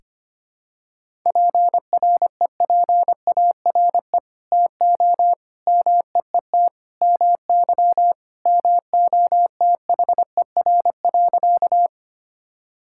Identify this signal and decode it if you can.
Morse code, heard as a keyed tone, about 25 wpm, 700 Hz. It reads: PREPARE TO MEET MY MOTHER.
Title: Sentences from Top 1000 Words - Review Part 6 - 25wpm